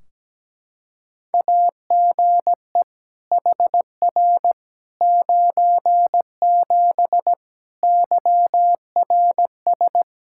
Morse code, heard as a keyed tone, about 17 wpm, 700 Hz. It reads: AGE HR 97 YRS
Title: Practice QSO - Exchange 3 - More Info - 17wpm